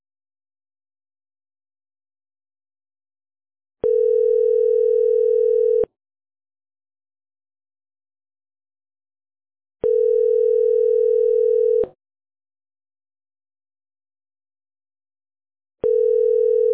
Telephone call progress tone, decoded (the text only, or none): ringback tone